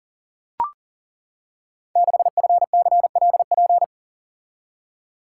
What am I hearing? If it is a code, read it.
Morse code, heard as a keyed tone, about 40 wpm, 690 Hz. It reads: 6FCLP
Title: Code Groups - 5 Letter:Numbers - 40wpm